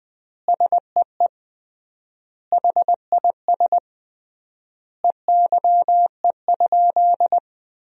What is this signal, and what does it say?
Morse code, heard as a keyed tone, about 20 wpm, 700 Hz. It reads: SEE HIS EYE?
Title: 3x Word-Spacing - Sentences from Top 300 Words - 20wpm